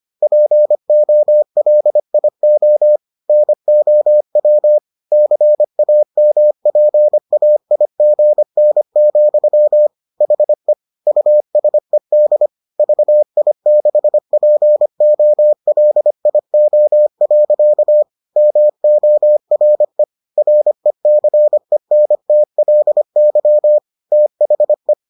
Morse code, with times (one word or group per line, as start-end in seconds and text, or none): 0.2-3.0 POLIO
3.3-4.8 NOW
5.1-9.9 CAMPAIGN,
10.2-10.7 HE
11.1-12.5 USED
12.8-18.0 VI6POLIO.
18.4-20.0 MORE
20.4-23.8 RECENTLY
24.1-24.9 THE